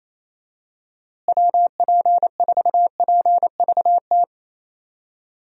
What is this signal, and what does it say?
Morse code, heard as a keyed tone, about 28 wpm, 700 Hz. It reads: WP4PVT
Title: USA Call Signs - 6 Characters - Repeated 2x - 28wpm